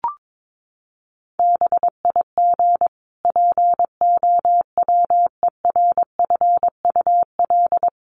Morse code, heard as a keyed tone, about 22 wpm, 700 Hz. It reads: BIG POWERFUL
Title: Sentences from Top 1300 Words 22wpm